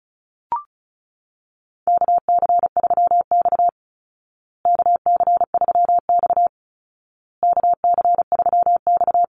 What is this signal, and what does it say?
Morse code, heard as a keyed tone, about 34 wpm, 700 Hz. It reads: KC3X KC3X KC3X
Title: Call Signs Repeated 2x - 35wpm